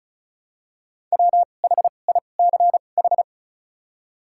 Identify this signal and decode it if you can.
Morse code, heard as a keyed tone, about 34 wpm, spaced as for 25 wpm, 700 Hz. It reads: WHICH
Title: 2x Character Spacing - Top 300 Words - 35wpm